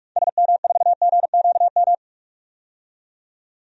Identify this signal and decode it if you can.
Morse code, heard as a keyed tone, about 42 wpm, 700 Hz. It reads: SM4GQM
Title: DXCC-100 Call Signs - 6 Characters - 45wpm